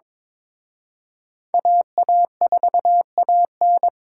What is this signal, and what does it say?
Morse code, heard as a keyed tone, about 26 wpm, 700 Hz. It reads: AA4AN
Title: Speed Racing - USA 2x2 Call Signs 22wpm